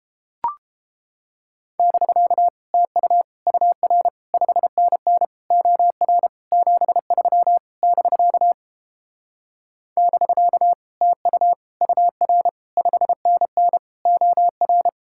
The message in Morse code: <BK> TU UR 5NN OR 73 <BK> <BK> TU UR 5NN OR

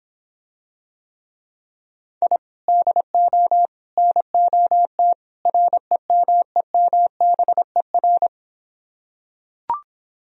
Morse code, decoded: I DO NOT REMEMBER

